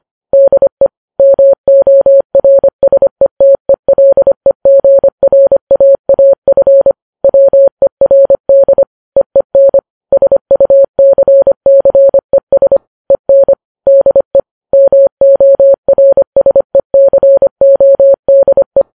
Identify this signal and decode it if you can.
Morse code, heard as a keyed tone, about 25 wpm, 550 Hz. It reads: DE MORSETELEGRAAF WERD EEN SUCCES EN DE MORSECODE